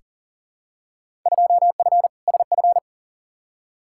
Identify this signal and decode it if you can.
Morse code, heard as a keyed tone, about 39 wpm, 700 Hz. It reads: 2F SF
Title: Field Day Exchanges - 40wpm